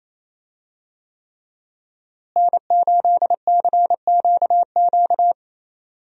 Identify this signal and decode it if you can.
Morse code, heard as a keyed tone, about 28 wpm, 700 Hz. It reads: N8CQQ
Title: USA Call Signs - 5 Characters - Repeated 3x - 28wpm